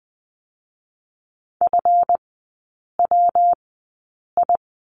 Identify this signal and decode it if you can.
Morse code, heard as a keyed tone, about 20 wpm, spaced as for 8 wpm, 700 Hz. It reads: FWI
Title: 138 Lesson 23e - Letter f - Three Characters 20wpm